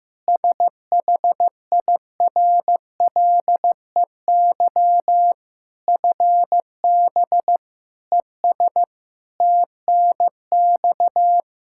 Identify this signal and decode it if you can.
Morse code, heard as a keyed tone, about 15 wpm, 700 Hz. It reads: SHIRLEY FB ES TNX
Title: Practice QSO - Exchange 2 - Info - 15wpm